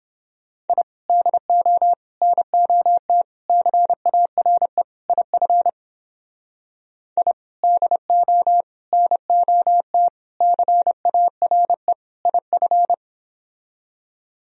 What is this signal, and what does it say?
Morse code, heard as a keyed tone, about 30 wpm, 700 Hz. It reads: I DO NOT CARE IF I DO NOT CARE IF